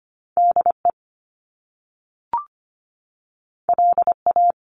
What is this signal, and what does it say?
Morse code, heard as a keyed tone, about 25 wpm, 700 Hz. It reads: DE LA
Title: Top Two Letter Combinations in Words 25wpm